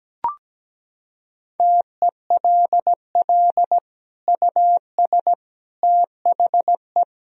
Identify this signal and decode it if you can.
Morse code, heard as a keyed tone, about 17 wpm, 700 Hz. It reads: TELL US THE
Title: Sentences from Top 1000 Words - Review Part 3 - 17wpm